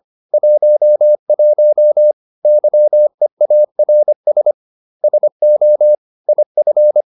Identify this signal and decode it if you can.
Morse code, heard as a keyed tone, about 25 wpm, 600 Hz. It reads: 11 YEARS SO IF
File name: SS311 25